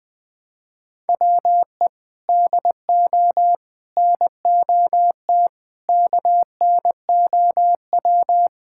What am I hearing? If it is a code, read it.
Morse code, heard as a keyed tone, about 21 wpm, 700 Hz. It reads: WE DO NOT KNOW
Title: Sentences with 5-Word Phrases - 20wpm